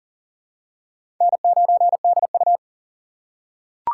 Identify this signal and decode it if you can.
Morse code, heard as a keyed tone, about 40 wpm, 700 Hz. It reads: N9DU
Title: USA Call Signs - 4 Characters - 6x Character Spacing - 40wpm